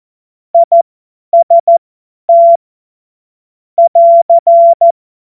Morse code, heard as a keyed tone, about 14 wpm, spaced as for 10 wpm, 680 Hz. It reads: IST <AR>
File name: Morsekurs / Witze / Witz 26. Oktober 2021, 50 F70